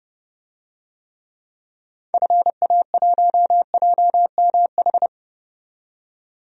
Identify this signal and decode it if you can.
Morse code, heard as a keyed tone, about 30 wpm, 700 Hz. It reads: FA1JMH
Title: DXCC-100 Call Signs - 6 Characters - 30wpm